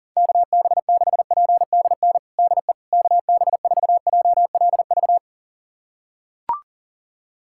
Morse code, heard as a keyed tone, about 40 wpm, 700 Hz. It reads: KB6PDN DE KB4JLV
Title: USA Callsign DE Callsign - 6 Characters - 40wpm